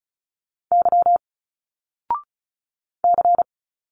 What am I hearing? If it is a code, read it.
Morse code, heard as a keyed tone, about 36 wpm, 700 Hz. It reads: Y C
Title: Single Letters 35wpm